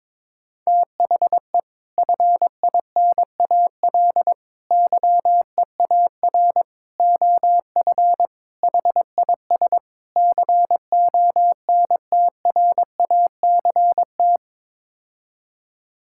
Morse code, heard as a keyed tone, about 22 wpm, 700 Hz. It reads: THE FINAL YEAR OF HIS CONTRACT
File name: practice / 22wpm / Sets of 6 Words - 22wpm